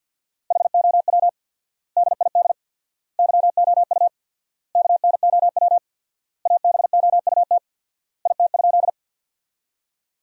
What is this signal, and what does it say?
Morse code, heard as a keyed tone, about 50 wpm, 700 Hz. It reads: HOW DID YOU KNOW ABOUT IT?
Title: 4x Word-Spacing - Sentences from Top 100 Words - 50wpm